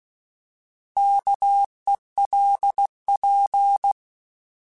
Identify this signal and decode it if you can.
Morse code, heard as a keyed tone, about 16 wpm, 800 Hz. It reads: KELP
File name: Alphabetical / Alphabetical mp3s / Ks @15